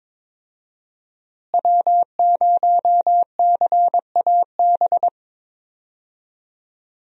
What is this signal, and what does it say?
Morse code, heard as a keyed tone, about 22 wpm, 700 Hz. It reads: W0CAB
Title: USA Call Signs - 5 Characters - Repeated 3x - 22wpm